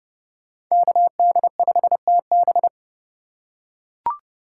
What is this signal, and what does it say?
Morse code, heard as a keyed tone, about 32 wpm, 700 Hz. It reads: KD5TB
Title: USA Call Signs - 5 Characters - 4x Character Spacing - 30wpm